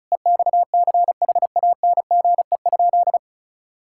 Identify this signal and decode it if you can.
Morse code, heard as a keyed tone, about 34 wpm, 700 Hz. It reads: EXCHANGE?
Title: Sentences from Top 1000 Words - Review Part 6 - 35wpm